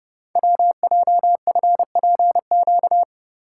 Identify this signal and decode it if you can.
Morse code, heard as a keyed tone, about 30 wpm, 690 Hz. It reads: WJFPQ
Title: Code Groups - 5 Letters - 30wpm